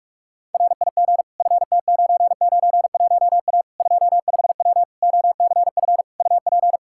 Morse code, heard as a keyed tone, about 46 wpm, 700 Hz. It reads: RIG FT991A 25W OCF UP
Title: Practice QSO - Exchange 2 - Info - 45wpm